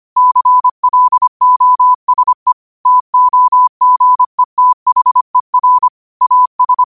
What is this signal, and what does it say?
Morse code, heard as a keyed tone, about 25 wpm, 1000 Hz. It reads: CLOSE TOGETHER AS
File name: qotdfiles / 25wpm